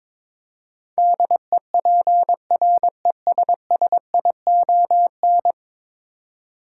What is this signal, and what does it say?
Morse code, heard as a keyed tone, about 21 wpm, 700 Hz. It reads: DEPRESSION
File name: practice / 22wpm / 10 Letter Words - First 3 Letters Given - 22wpm